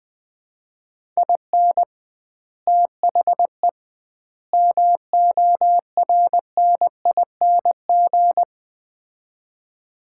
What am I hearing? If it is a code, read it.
Morse code, heard as a keyed tone, about 20 wpm, 700 Hz. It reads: IN THE MORNING
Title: 2x Word-Spacing - Sets of 3 Words - 20wpm